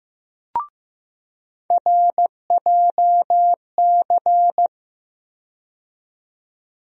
Morse code, heard as a keyed tone, about 15 wpm, 700 Hz. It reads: RJC
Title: USA Call Sign Postfixes - 3 Letters 15wpm